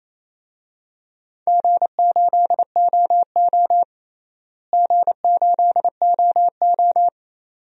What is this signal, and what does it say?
Morse code, heard as a keyed tone, about 28 wpm, 700 Hz. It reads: G8OO G8OO
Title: DXCC-100 Call Signs - 4 Characters - Repeated 2x - 28wpm